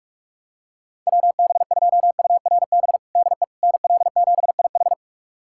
Morse code, heard as a keyed tone, about 47 wpm, 700 Hz. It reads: WB2URB DE NL7IH
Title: USA Callsign DE Callsign 45wpm